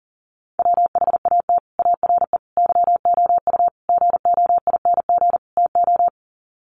Morse code, heard as a keyed tone, about 40 wpm, 700 Hz. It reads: WHAT ARE YOU GOING TO
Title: Sets of 5 Words 40wpm